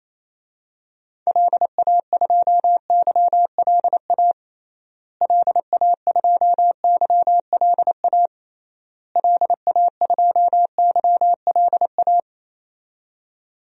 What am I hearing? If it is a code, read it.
Morse code, heard as a keyed tone, about 28 wpm, 700 Hz. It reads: LA2YLA LA2YLA LA2YLA
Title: DXCC-100 Call Signs - 6 Characters - Repeated 3x - 28wpm